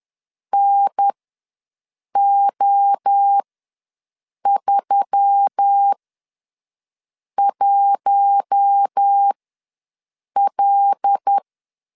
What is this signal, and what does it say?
Morse code, heard as a keyed tone, about 10 wpm, spaced as for 6 wpm, 800 Hz. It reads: NO31L
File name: LEZIONI / 5 LBVC / 5 LBV ws 10wpm 3645